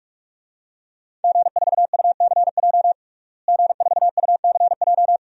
Morse code, heard as a keyed tone, about 46 wpm, 700 Hz. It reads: G4UCJ G4UCJ
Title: DXCC-100 Call Signs - 5 Characters - Repeated 2x - 45wpm